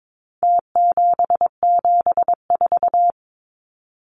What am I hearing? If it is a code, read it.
Morse code, heard as a keyed tone, about 22 wpm, 700 Hz. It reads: T774
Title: Numbers - Contest Style - 4-Digit 22wpm